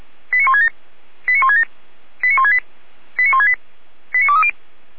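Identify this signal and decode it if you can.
Five-tone selective call (ZVEI), heard as quick runs of five tones, tones of about 70 ms. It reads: 89167 89167 89168 89168 89230
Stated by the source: ZVEI five-tone selective call received on SDR.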